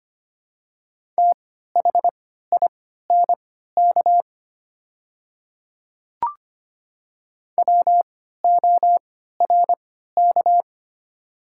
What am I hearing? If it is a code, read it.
Morse code, heard as a keyed tone, about 25 wpm, spaced as for 14 wpm, 700 Hz. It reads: THINK WORK
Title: 3x Character Spacing - Top 200 Words - 25wpm